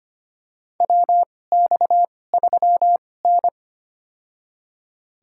Morse code, heard as a keyed tone, about 25 wpm, spaced as for 18 wpm, 700 Hz. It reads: WX3N
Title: USA Call Signs - 2x Character Spacing - 25wpm